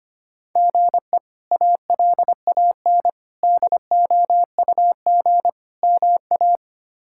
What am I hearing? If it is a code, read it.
Morse code, heard as a keyed tone, about 25 wpm, 700 Hz. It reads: GE ALAN DOUG MA
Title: K1USN SST QSO Practice - 25wpm